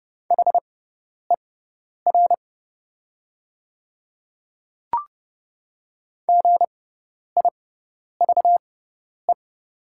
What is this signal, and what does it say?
Morse code, heard as a keyed tone, about 30 wpm, spaced as for 10 wpm, 700 Hz. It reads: HER GIVE
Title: 6x Character Spacing - Top 200 Words - 30wpm